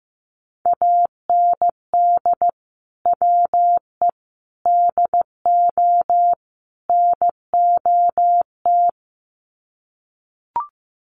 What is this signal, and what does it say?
Morse code, heard as a keyed tone, about 15 wpm, 700 Hz. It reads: AND WE DO NOT
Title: Sets of 4 Words 15wpm